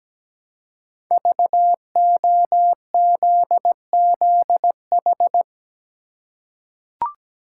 Morse code, heard as a keyed tone, about 17 wpm, 700 Hz. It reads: VOZZH
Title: Code Groups - 5 Letter:Numbers - 17wpm